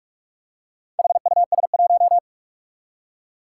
Morse code, heard as a keyed tone, about 46 wpm, 700 Hz. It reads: HUS1
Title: Four Challenging Letter-Number Combinations 45wpm